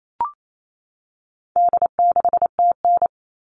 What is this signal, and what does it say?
Morse code, heard as a keyed tone, about 28 wpm, 700 Hz. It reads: D6TN
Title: All Cut Numbers - 4-Digit 28wpm